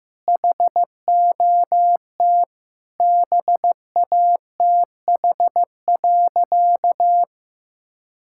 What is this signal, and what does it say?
Morse code, heard as a keyed tone, about 15 wpm, 700 Hz. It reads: HOT BATH.
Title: Sentences from Top 1600 Words 15wpm